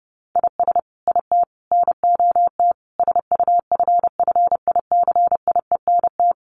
Morse code, heard as a keyed tone, about 28 wpm, 700 Hz. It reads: IS IT NOT SUFFICIENT
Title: Sentences from Top 1400 Words 30wpm